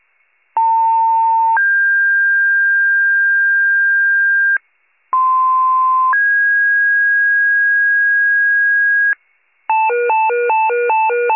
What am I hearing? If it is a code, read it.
Two-tone page, heard as a one-second tone, then a three-second tone, about 900 then 1600 Hz, about 1040 then 1650 Hz. It is heard from 0.6 to 9.1 s.